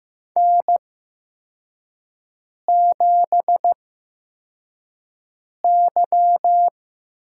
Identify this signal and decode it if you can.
Morse code, heard as a keyed tone, about 15 wpm, spaced as for 4 wpm, 700 Hz. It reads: N7Y
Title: USA Call Signs - 5 Characters - 8x Character Spacing - 15wpm